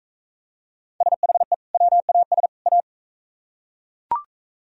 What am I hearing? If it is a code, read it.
Morse code, heard as a keyed tone, about 45 wpm, 700 Hz. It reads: SHE WAS A